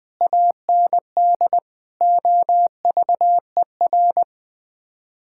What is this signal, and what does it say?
Morse code, heard as a keyed tone, about 20 wpm, 700 Hz. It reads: AND OVER
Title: Sets of 6 Words - 20wpm